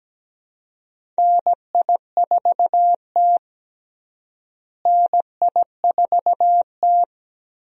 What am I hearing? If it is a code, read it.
Morse code, heard as a keyed tone, about 17 wpm, 700 Hz. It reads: NI4T NI4T
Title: USA Call Signs - 4 Characters - Repeated 2x - 17wpm